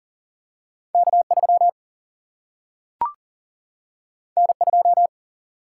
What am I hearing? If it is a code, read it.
Morse code, heard as a keyed tone, about 39 wpm, 700 Hz. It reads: K3 N2
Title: Single-Letter Single-Number 40wpm